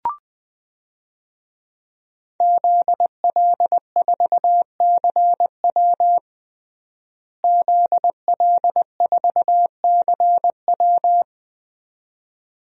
Morse code, heard as a keyed tone, about 21 wpm, 700 Hz. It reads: ZL4CW ZL4CW
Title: DXCC-100 Call Signs - 5 Characters - Repeated 2x - 20wpm